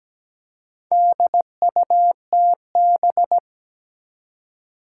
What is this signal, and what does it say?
Morse code, heard as a keyed tone, about 17 wpm, 700 Hz. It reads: DUTB
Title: Four Challenging Letter-Number Combinations 17wpm